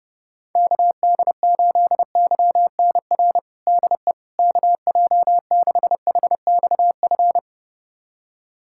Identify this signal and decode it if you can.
Morse code, heard as a keyed tone, about 30 wpm, 700 Hz. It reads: KD8YNR DE KJ6HXF
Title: USA Callsign DE Callsign 30wpm